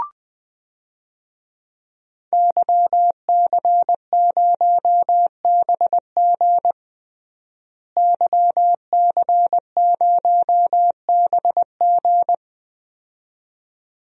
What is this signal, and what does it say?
Morse code, heard as a keyed tone, about 20 wpm, 700 Hz. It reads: YC0BG YC0BG
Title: DXCC-100 Call Signs - 5 Characters - Repeated 2x - 20wpm